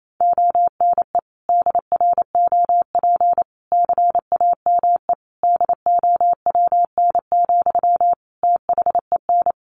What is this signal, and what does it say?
Morse code, heard as a keyed tone, about 28 wpm, 700 Hz. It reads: ONE DROP CAME DOWN, THEN
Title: Sentences from Top 1400 Words 28wpm